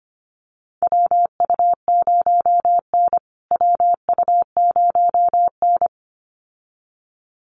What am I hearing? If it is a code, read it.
Morse code, heard as a keyed tone, about 25 wpm, 700 Hz. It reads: WU0N WU0N
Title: USA Call Signs Repeated 2x 25wpm